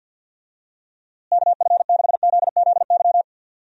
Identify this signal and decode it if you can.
Morse code, heard as a keyed tone, about 51 wpm, 700 Hz. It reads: KF6ZZY
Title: USA Call Signs - 6 Characters - Repeated 3x - 50wpm